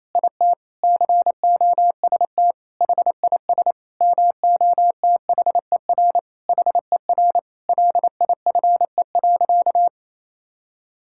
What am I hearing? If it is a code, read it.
Morse code, heard as a keyed tone, about 28 wpm, 700 Hz. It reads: IT COST HIS MOTHER HER LIFE.
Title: Sentences from Top 1000 Words - Review Part 5 - 28wpm